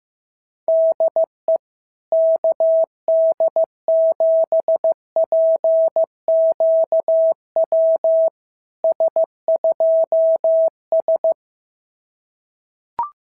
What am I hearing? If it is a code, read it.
Morse code, heard as a keyed tone, about 15 wpm, 650 Hz. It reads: DE KD7PQW S2S
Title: SOTA QSO Practice 15wpm